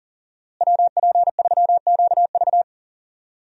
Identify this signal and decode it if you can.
Morse code, heard as a keyed tone, about 37 wpm, 700 Hz. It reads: WP3QV
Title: USA Call Signs - 5 Characters - Repeated 3x - 40wpm